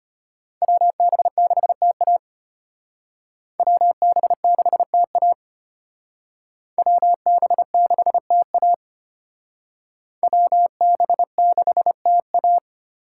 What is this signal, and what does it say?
Morse code, heard as a keyed tone, about 32 wpm, 700 Hz. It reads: WB6TA WB6TA WB6TA WB6TA